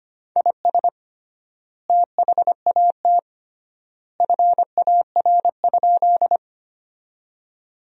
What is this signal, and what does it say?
Morse code, heard as a keyed tone, about 25 wpm, 700 Hz. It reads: IS THAT FAR?
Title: 3x Word-Spacing - Sentences from Top 300 Words - 25wpm